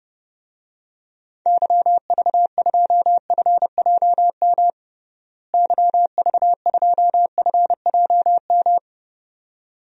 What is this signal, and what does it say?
Morse code, heard as a keyed tone, about 30 wpm, 700 Hz. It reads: YV2FJM YV2FJM
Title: DXCC-100 Call Signs - 6 Characters - Repeated 2x - 30wpm